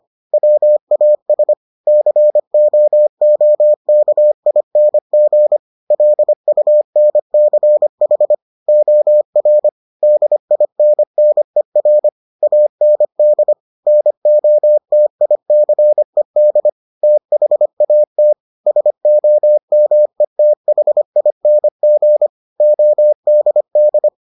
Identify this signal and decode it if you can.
Morse code, heard as a keyed tone, about 25 wpm, 600 Hz. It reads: WAS COOKING LUNCH OR DINNER AND NOTICED THAT SOMETHING ODD